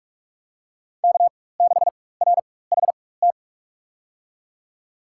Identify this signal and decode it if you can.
Morse code, heard as a keyed tone, about 47 wpm, spaced as for 21 wpm, 700 Hz. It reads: K6RHT
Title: USA Call Signs - 4x Character Spacing - 45wpm